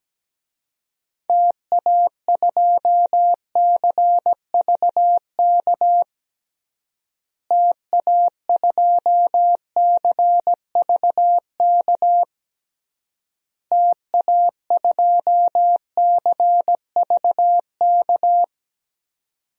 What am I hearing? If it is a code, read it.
Morse code, heard as a keyed tone, about 17 wpm, 700 Hz. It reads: TA2CVK TA2CVK TA2CVK
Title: DXCC-100 Call Signs - 6 Characters - Repeated 3x - 17wpm